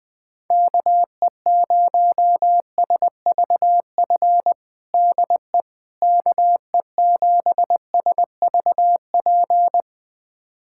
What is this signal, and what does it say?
Morse code, heard as a keyed tone, about 20 wpm, 700 Hz. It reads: KE0SVF DE KE7SVP
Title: USA Callsign DE Callsign - 6 Characters - 20wpm